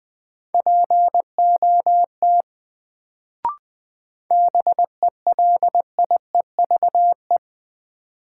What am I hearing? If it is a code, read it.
Morse code, heard as a keyed tone, about 20 wpm, 700 Hz. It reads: POT BELIEVE
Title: Sentences from Top 1000 Words - New Word Drill - 20wpm